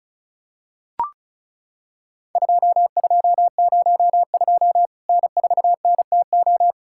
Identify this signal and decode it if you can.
Morse code, heard as a keyed tone, about 35 wpm, 700 Hz. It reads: 2202 N4NTO
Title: Worked All Europe DX Contest - QTC Exchange - 35wpm